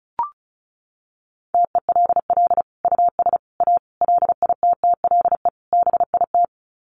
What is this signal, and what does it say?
Morse code, heard as a keyed tone, about 34 wpm, 700 Hz. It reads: TELL US A LITTLE BIT
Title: Sets of 5 Words Encore2 35wpm